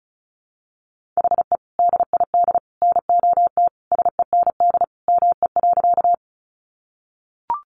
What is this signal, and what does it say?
Morse code, heard as a keyed tone, about 34 wpm, 700 Hz. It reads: HE DID NOT SEND ME.
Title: Sentences from Top 1500 Words 35wpm